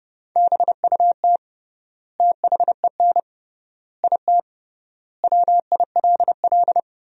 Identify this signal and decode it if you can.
Morse code, heard as a keyed tone, about 30 wpm, 700 Hz. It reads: BUT THEN IT WILL
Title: 3x Word-Spacing - Sentences from Top 300 Words - 30wpm